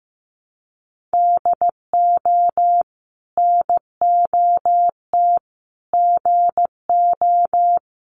Morse code, heard as a keyed tone, about 15 wpm, 700 Hz. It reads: DO NOT GO
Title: Sets of 4 Words 15wpm